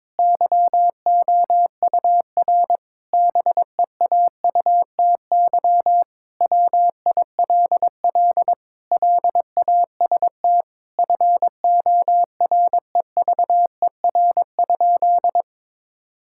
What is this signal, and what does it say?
Morse code, heard as a keyed tone, about 22 wpm, 700 Hz. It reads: YOUR BEAUTY WILL LAST FOREVER?
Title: Sentences from Top 1600 Words 22wpm